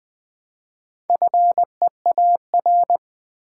Morse code, heard as a keyed tone, about 20 wpm, 700 Hz. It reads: FEAR
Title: Build A Word - 2 Letters At A Time - 4-Letter Words - 20wpm